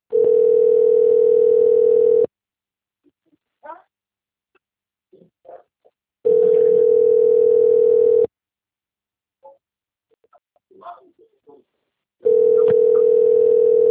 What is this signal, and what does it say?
Ringback tone (North America), 440 + 480 Hz.
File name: rg-1600-unknown- - - .67694